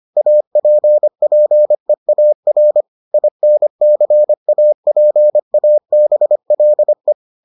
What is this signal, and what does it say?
Morse code, heard as a keyed tone, about 25 wpm, 600 Hz. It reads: APPEAR INCAPABLE